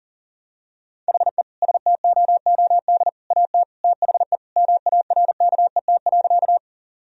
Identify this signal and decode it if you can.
Morse code, heard as a keyed tone, about 40 wpm, 700 Hz. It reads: HE STOOD AT THE MARKET.
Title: Sentences from Top 1000 Words - Review Part 3 - 40wpm